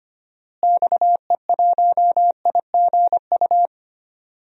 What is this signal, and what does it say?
Morse code, heard as a keyed tone, about 25 wpm, 700 Hz. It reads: XE1IGU
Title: DXCC-100 Call Signs - 6 Characters - Repeated 3x - 25wpm